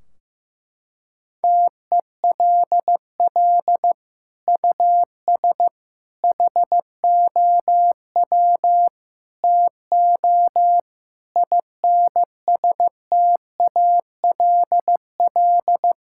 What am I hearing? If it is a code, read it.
Morse code, heard as a keyed tone, about 15 wpm, 700 Hz. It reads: TELL US HOW TO INSTALL
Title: Sentences from Top 1600 Words 15wpm